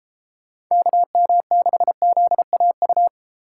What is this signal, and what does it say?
Morse code, heard as a keyed tone, about 33 wpm, 700 Hz. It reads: KM6ZAU